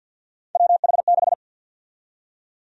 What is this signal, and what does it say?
Morse code, heard as a keyed tone, about 52 wpm, 700 Hz. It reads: WH6
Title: USA Call Sign Prefixes - 50wpm